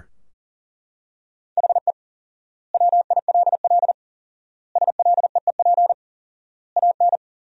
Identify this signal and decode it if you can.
Morse code, heard as a keyed tone, about 37 wpm, 700 Hz. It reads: HE WILL SLEEP AN
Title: 4x Word-Spacing - Sentences from Top 500 Words - 40wpm